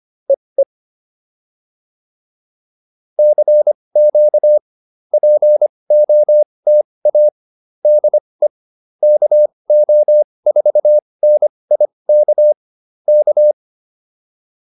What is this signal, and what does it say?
Morse code, heard as a keyed tone, about 25 wpm, spaced as for 20 wpm, 600 Hz. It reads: EE CQ POTA DE KO4NIK K